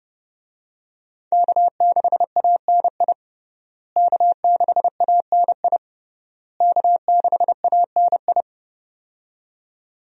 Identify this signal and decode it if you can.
Morse code, heard as a keyed tone, about 32 wpm, 700 Hz. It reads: K6ANI K6ANI K6ANI
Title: USA Call Signs - 5 Characters - Repeated 3x - 30wpm